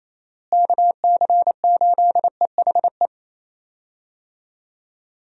Morse code, heard as a keyed tone, about 28 wpm, 700 Hz. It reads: KC8EHE
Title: USA Call Signs - 6 Characters - Repeated 3x - 28wpm